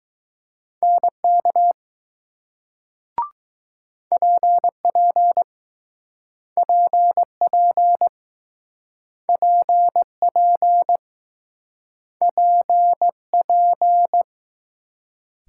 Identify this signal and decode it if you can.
Morse code, heard as a keyed tone, about 20 wpm, 700 Hz. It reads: NK PP PP PP PP